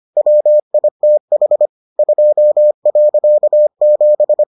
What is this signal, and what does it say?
Morse code, heard as a keyed tone, about 25 wpm, 600 Hz. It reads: WITH 2.7